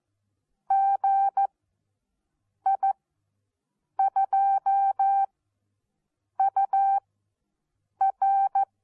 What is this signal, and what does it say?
Morse code, heard as a keyed tone, about 14 wpm, spaced as for 6 wpm, 780 Hz. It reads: GI2UR
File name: Onlinelehrgang / morsen / DJ4UF Morsen14